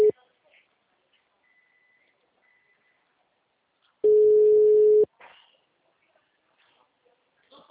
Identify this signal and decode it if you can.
Ringback tone (Europe, 425 Hz), with speech.